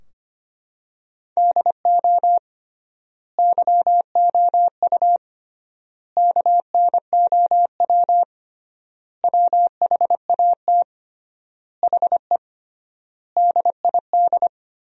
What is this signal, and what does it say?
Morse code, heard as a keyed tone, about 25 wpm, 700 Hz. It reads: DO YOU KNOW WHAT HE DID
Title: 3x Word-Spacing - Sentences from Top 300 Words - 25wpm